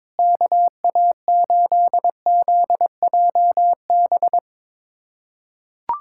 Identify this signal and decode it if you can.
Morse code, heard as a keyed tone, about 22 wpm, 700 Hz. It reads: KA8ZJB
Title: USA Callsign DE Callsign - 6 Characters - 22wpm